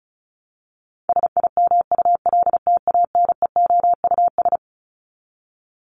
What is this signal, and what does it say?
Morse code, heard as a keyed tone, about 34 wpm, 700 Hz. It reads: SIMULTANEOUS
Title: LY Suffix - Top 100 Words with Base Words 35wpm